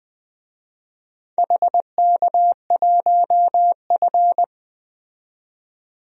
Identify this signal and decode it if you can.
Morse code, heard as a keyed tone, about 20 wpm, 700 Hz. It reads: HK1F
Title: DXCC-100 Call Signs - 4 Characters - 20wpm